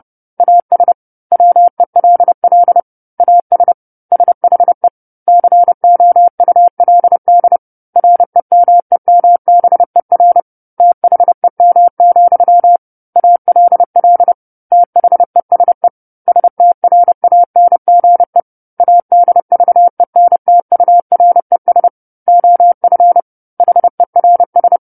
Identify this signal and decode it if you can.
Morse code, heard as a keyed tone, about 30 wpm, 700 Hz. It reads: AS WELL AS SHE COULD REMEMBER THEM, ALL THESE STRANGE ADVENTURES OF HERS
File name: 10-Chapter-10 30 WPM